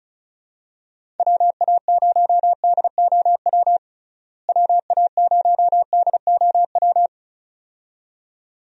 Morse code, heard as a keyed tone, about 34 wpm, 700 Hz. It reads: WA0DOW WA0DOW
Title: USA Call Signs - 6 Characters - Repeated 2x - 35wpm